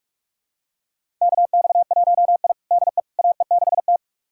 Morse code, heard as a keyed tone, about 44 wpm, 700 Hz. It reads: KX1I DE AE6T
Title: USA Callsign DE Callsign - 4 Characters - 45wpm